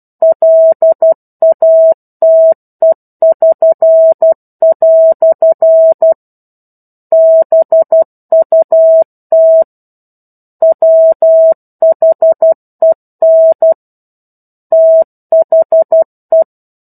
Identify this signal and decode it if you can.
Morse code, heard as a keyed tone, about 12 wpm, 650 Hz. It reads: LATE<SN>" BUT WHEN THE
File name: 01-Chapter-01 12 WPM